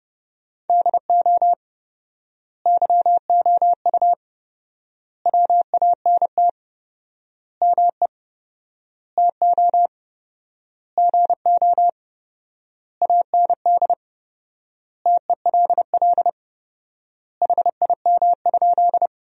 Morse code, heard as a keyed tone, about 30 wpm, 700 Hz. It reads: DO YOU WANT ME TO GO AND TELL HIM?
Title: 4x Word-Spacing - Sentences from Top 200 Words - 30wpm